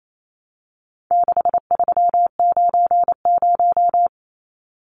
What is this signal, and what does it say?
Morse code, heard as a keyed tone, about 28 wpm, 700 Hz. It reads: 6390
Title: Numbers - 4-Digit 28wpm